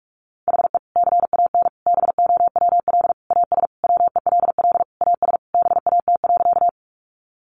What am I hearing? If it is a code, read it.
Morse code, heard as a keyed tone, about 44 wpm, 700 Hz. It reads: HE CAN BOWL AS WELL AS BAT.
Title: Sentences from Top 1500 Words 45wpm